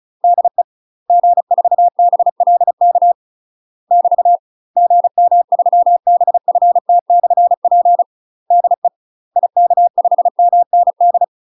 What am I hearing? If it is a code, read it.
Morse code, heard as a keyed tone, about 36 wpm, 700 Hz. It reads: DE G4BLK <BT> GM3BFT/P DE IK5MND
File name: UK MASTER QSOs TRAINING / 35WPM / UK MASTER QSOs 0013